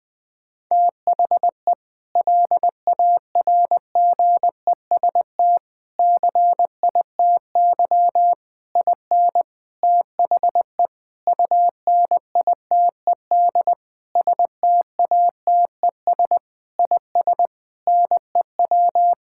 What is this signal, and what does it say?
Morse code, heard as a keyed tone, about 21 wpm, 700 Hz. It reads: THE LARGEST CITY IN THE UNITED STATES IS NEW
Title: Sentences with 5-Word Phrases - 20wpm